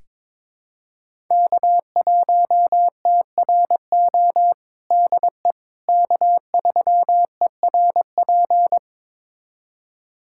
Morse code, heard as a keyed tone, about 22 wpm, 700 Hz. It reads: K1TRO DE K3ERP
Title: USA Callsign DE Callsign 22wpm